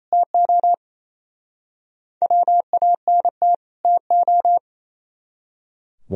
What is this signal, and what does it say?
Morse code, heard as a keyed tone, about 34 wpm, 690 Hz. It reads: TO WANT TO
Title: Speed-Racing - Sets of 2 Words 28wpm